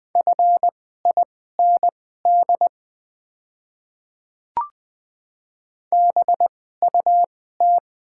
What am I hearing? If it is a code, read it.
Morse code, heard as a keyed tone, about 20 wpm, spaced as for 14 wpm, 700 Hz. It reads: FIND BUT
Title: 2x Character Spacing - Top 200 Words - 20wpm